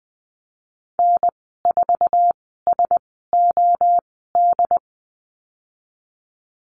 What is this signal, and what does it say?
Morse code, heard as a keyed tone, about 20 wpm, spaced as for 15 wpm, 700 Hz. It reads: N4SOD
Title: 069 Lesson 12g - Letter d - Call Signs 20wpm